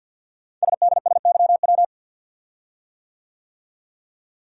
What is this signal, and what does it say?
Morse code, heard as a keyed tone, about 50 wpm, 700 Hz. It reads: SDSYW
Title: Code Groups - 5 Letter:Numbers - 50wpm